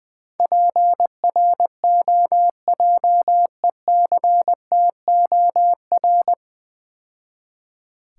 Morse code, heard as a keyed tone, about 20 wpm, 700 Hz. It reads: PROJECTOR
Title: Easy Words - Speed-Racing with Hints - 20wpm